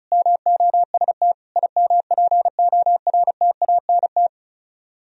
Morse code, heard as a keyed tone, about 43 wpm, 700 Hz. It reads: MOST IMPORTANT
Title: Speed-Racing - Sets of 5 Words 35wpm